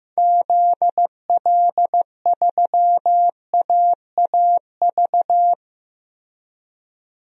Morse code, heard as a keyed tone, about 15 wpm, 700 Hz. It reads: ZL3AAV
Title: DXCC-100 Call Signs - 6 Characters - Repeated 2x - 15wpm